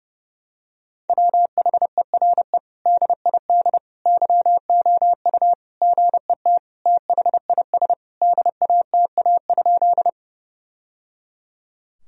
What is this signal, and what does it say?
Morse code, heard as a keyed tone, about 32 wpm, 700 Hz. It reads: WHERE DID YOU GET THIS DATA?
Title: Sentences from Top 1000 Words - Review Part 3 - 30wpm